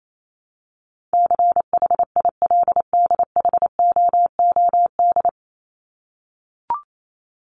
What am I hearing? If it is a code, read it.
Morse code, heard as a keyed tone, about 28 wpm, 700 Hz. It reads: CHILDHOOD
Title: Top 1751-2000 Words 28wpm